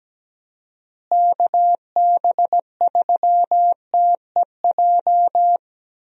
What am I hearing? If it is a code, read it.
Morse code, heard as a keyed tone, about 17 wpm, 700 Hz. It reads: KB3TEJ
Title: USA Call Signs - 6 Characters - Repeated 2x - 17wpm